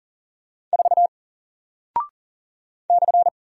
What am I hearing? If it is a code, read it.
Morse code, heard as a keyed tone, about 39 wpm, 700 Hz. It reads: 4 /
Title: Single Character (Letter:Number:Punctuation:Prosign) Alt 40wpm